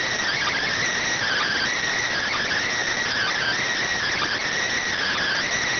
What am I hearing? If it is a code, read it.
Emergency Alert System (SAME) tones (header bursts), heard as harsh buzzing bursts. It is heard from 1.2 to 5.6 s.